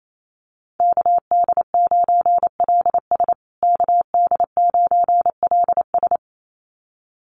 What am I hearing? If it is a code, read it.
Morse code, heard as a keyed tone, about 28 wpm, 700 Hz. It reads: KD9LS KD9LS
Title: USA Call Signs Repeated 2x 28wpm